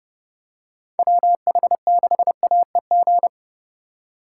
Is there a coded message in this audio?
Yes: Morse code from 1.0 to 3.3 s, heard as a keyed tone, about 30 wpm, 700 Hz. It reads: WH6AEG